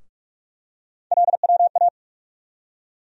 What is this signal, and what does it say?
Morse code, heard as a keyed tone, about 46 wpm, 700 Hz. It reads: LWA
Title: USA Call Sign Postfixes - 3 Letters 45wpm